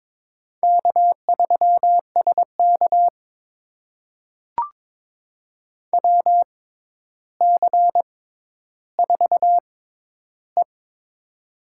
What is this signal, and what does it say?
Morse code, heard as a keyed tone, about 22 wpm, spaced as for 11 wpm, 700 Hz. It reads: K3SK WC4E
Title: USA Call Signs - 4 Characters - 6x Character Spacing - 22wpm